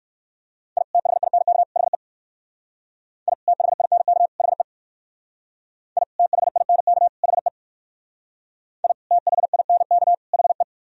Morse code, heard as a keyed tone, about 58 wpm, 700 Hz. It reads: I THINK HE I THINK HE I THINK HE I THINK HE